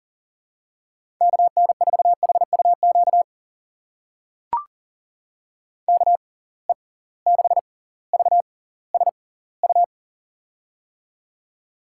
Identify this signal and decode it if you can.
Morse code, heard as a keyed tone, about 38 wpm, spaced as for 20 wpm, 700 Hz. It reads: KN4HUQ KE6VSU
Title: USA Call Signs - 6 Characters - 6x Character Spacing - 40wpm